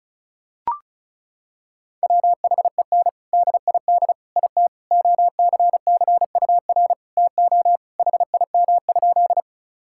Morse code, heard as a keyed tone, about 34 wpm, 700 Hz. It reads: WHEN DID IT OCCUR TO HIM?
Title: Sentences from Top 1000 Words - Review Part 6 - 35wpm